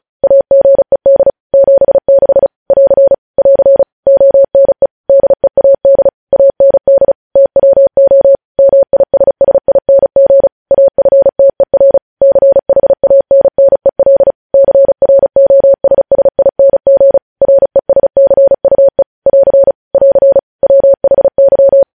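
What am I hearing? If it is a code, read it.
Morse code, heard as a keyed tone, about 36 wpm, 550 Hz. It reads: AGED 76 <AR> <AR> ONE DEAD AND TWO MISSING AFTER CHANNEL CROSSING RESCUE <AR> <AR> WHY
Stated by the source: News headlines sent in Morse code at 35 WPM.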